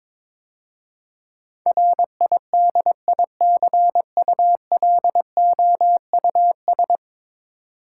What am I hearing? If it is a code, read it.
Morse code, heard as a keyed tone, about 22 wpm, 700 Hz. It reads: RIDICULOUS
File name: practice / 22wpm / 10 Letter Words - First 3 Letters Given - 22wpm